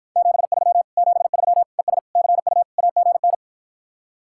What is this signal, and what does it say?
Morse code, heard as a keyed tone, about 52 wpm, 700 Hz. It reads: 73 73 ES CU AGN